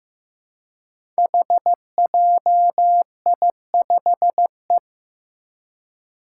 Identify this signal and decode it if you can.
Morse code, heard as a keyed tone, about 15 wpm, 700 Hz. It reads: HJI5E
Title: Five Challenging Letter-Number Combinations 15wpm